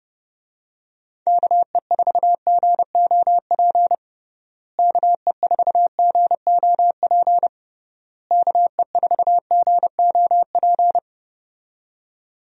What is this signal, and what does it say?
Morse code, heard as a keyed tone, about 30 wpm, 700 Hz. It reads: KE4GOP KE4GOP KE4GOP
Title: USA Call Signs - 6 Characters - Repeated 3x - 30wpm